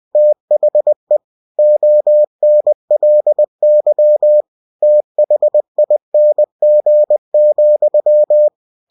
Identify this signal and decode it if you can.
Morse code, heard as a keyed tone, about 20 wpm, 600 Hz. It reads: THE ONLY THING,
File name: cw-books / winnie-the-pooh / winnie-20wpm-page-0085